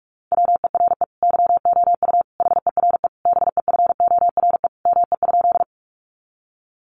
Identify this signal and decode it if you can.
Morse code, heard as a keyed tone, about 44 wpm, 700 Hz. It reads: WERE YOU HERE BEFORE ME?
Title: Sentences from Top 200 Words 45wpm